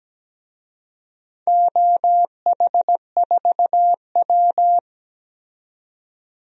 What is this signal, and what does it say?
Morse code, heard as a keyed tone, about 17 wpm, 700 Hz. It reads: OH4W
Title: DXCC-100 Call Signs - 4 Characters - 17wpm